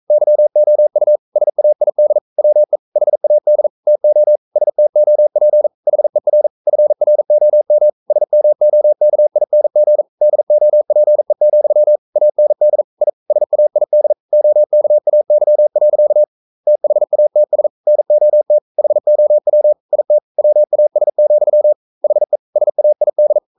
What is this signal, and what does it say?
Morse code, heard as a keyed tone, about 42 wpm, 600 Hz. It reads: YOU SAID WE HAD TO STOP HER FROM SMOKING DOPE, AND I SAID OKAY. THATS NOT HOW IT WAS, HE SAID
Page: Romanen Never af Ken Follet på engelsk.